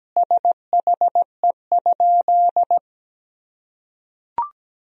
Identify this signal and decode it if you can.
Morse code, heard as a keyed tone, about 17 wpm, 700 Hz. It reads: SHE?
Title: Sentences from Top 200 Words and Abbreviations 17wpm